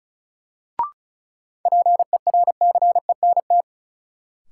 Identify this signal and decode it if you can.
Morse code, heard as a keyed tone, about 34 wpm, 700 Hz. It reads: PERCENT
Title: Sets of 2 Words - New Word Drill - 35wpm